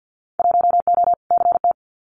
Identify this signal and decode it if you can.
Morse code, heard as a keyed tone, about 50 wpm, 700 Hz. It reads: 1O CT
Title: Winter Field Day Exchanges 50wpm